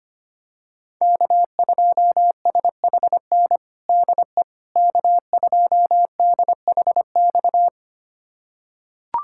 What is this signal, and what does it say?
Morse code, heard as a keyed tone, about 25 wpm, 700 Hz. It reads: K2SHN DE K2DHX
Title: USA Callsign DE Callsign - 5 Characters - 25wpm